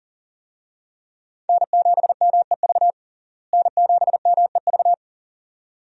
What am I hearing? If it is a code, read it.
Morse code, heard as a keyed tone, about 40 wpm, 700 Hz. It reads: N7MEV N7MEV
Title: USA Call Signs - 5 Characters - Repeated 2x - 40wpm